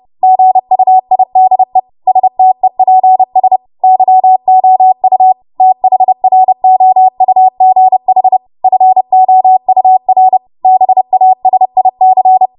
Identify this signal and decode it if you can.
Morse code, heard as a keyed tone, about 30 wpm, 750 Hz. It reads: GUIDE STEPS YOU THROUGH FOUR BASIC
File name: Morse / 30WPM